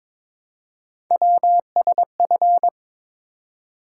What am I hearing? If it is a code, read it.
Morse code, heard as a keyed tone, about 22 wpm, 700 Hz. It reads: WSF
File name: practice / 22wpm / Three Letter-Number Combinations - 22wpm